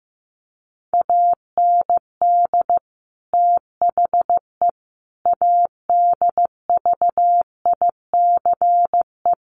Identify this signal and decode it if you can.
Morse code, heard as a keyed tone, about 15 wpm, 700 Hz. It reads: AND THE ADVICE
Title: Sentences from Top 1100 Words 15wpm